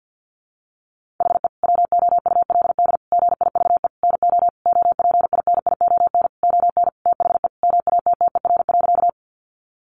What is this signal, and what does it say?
Morse code, heard as a keyed tone, about 50 wpm, 700 Hz. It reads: HE WOULD GIVE NO OPINION ON THE MATTER.
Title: Sentences from Top 1200 Words 50wpm